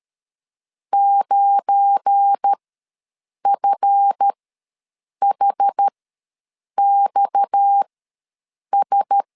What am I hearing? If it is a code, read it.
Morse code, heard as a keyed tone, about 13 wpm, spaced as for 7 wpm, 800 Hz. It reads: 9FHXS
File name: LEZIONI / 10 SIGNS / 10A SLASH ws 12wpm 3443